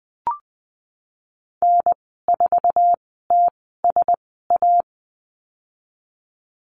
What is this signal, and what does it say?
Morse code, heard as a keyed tone, about 20 wpm, spaced as for 14 wpm, 700 Hz. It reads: N4TSA
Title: 069 Lesson 12g - Letter d - Call Signs 20wpm